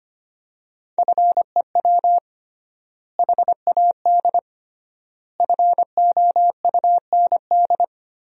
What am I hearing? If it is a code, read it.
Morse code, heard as a keyed tone, about 25 wpm, 700 Hz. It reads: FEW HAD FOUND
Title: 3x Word-Spacing - Sentences from Top 300 Words - 25wpm